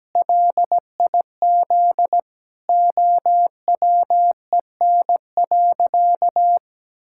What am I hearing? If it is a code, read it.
Morse code, heard as a keyed tone, about 17 wpm, 700 Hz. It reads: LIZ OWEN.
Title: Pangram Challenge v2 - 17wpm